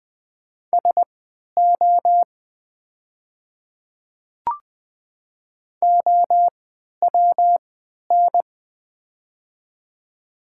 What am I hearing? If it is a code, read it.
Morse code, heard as a keyed tone, about 20 wpm, spaced as for 11 wpm, 700 Hz. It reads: SO OWN
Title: 3x Character Spacing - Top 300 Words - 20wpm